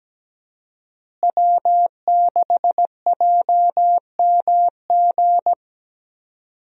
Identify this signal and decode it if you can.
Morse code, heard as a keyed tone, about 17 wpm, 700 Hz. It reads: W6JMG
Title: USA Call Signs - 5 Characters - Repeated 2x - 17wpm